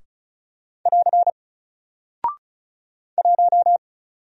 Morse code, heard as a keyed tone, about 36 wpm, 700 Hz. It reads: <AR> 1
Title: Single Character (Letter:Number:Punctuation:Prosign) 35wpm